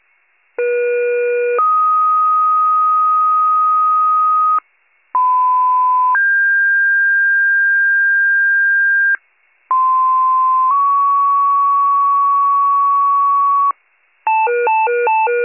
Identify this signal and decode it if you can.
Two-tone page, heard as a one-second tone, then a three-second tone, about 480 then 1250 Hz, about 1010 then 1600 Hz, about 1040 then 1120 Hz. It is heard from 0.6 to 13.7 s.